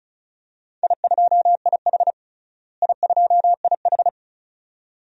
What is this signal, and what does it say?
Morse code, heard as a keyed tone, about 36 wpm, 700 Hz. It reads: I2IH I2IH
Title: DXCC-100 Call Signs - 4 Characters - Repeated 2x - 35wpm